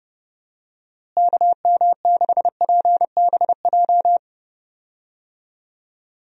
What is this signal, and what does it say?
Morse code, heard as a keyed tone, about 30 wpm, 700 Hz. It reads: KM6PBJ
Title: USA Call Signs - 6 Characters - Repeated 2x - 30wpm